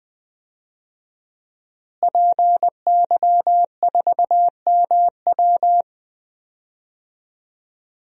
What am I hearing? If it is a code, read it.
Morse code, heard as a keyed tone, about 21 wpm, 700 Hz. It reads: PY4MW
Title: DXCC-100 Call Signs - 5 Characters - Repeated 3x - 20wpm